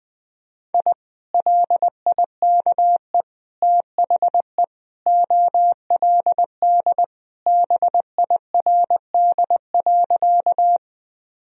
Morse code, heard as a keyed tone, about 20 wpm, 700 Hz. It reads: I LIKE THE OLD BIRD.
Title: Sentences from Top 1000 Words - Review Part 2 - 20wpm